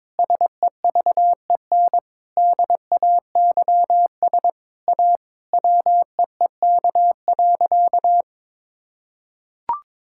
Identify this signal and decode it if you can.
Morse code, heard as a keyed tone, about 22 wpm, 700 Hz. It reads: SEVEN DAYS A WEEK.
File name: Sentences with 7-Word Phrases - 22wpm